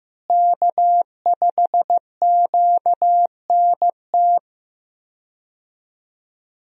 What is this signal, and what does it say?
Morse code, heard as a keyed tone, about 15 wpm, 700 Hz. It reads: K5QNT
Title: USA Call Signs - 5 Characters - Repeated 2x - 15wpm